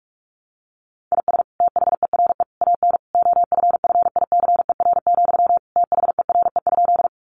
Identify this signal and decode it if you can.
Morse code, heard as a keyed tone, about 44 wpm, 700 Hz. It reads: IS THERE AN OFFICER, THERE?
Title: Sentences from Top 1200 Words 45wpm